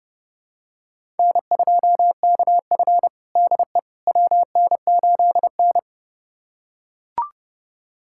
Morse code, heard as a keyed tone, about 30 wpm, 700 Hz. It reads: N2KF DE WN8N
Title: USA Callsign DE Callsign - 4 Characters - 30wpm